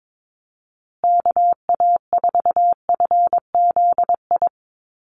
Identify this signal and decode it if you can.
Morse code, heard as a keyed tone, about 22 wpm, 700 Hz. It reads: KA4FZI
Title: USA Call Signs Repeated 2x 22wpm